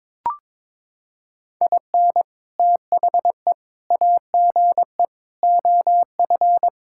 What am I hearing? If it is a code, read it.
Morse code, heard as a keyed tone, about 22 wpm, 700 Hz. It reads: IN THE AGE OF
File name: practice / 22wpm / Sentences with 3-Word Phrases - 22wpm